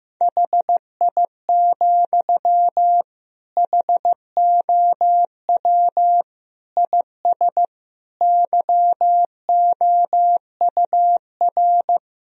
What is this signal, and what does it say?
Morse code, heard as a keyed tone, about 15 wpm, 700 Hz. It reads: HI, HOW IS YOUR
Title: Sentences from Top 1000 Words - Review Part 3 - 15wpm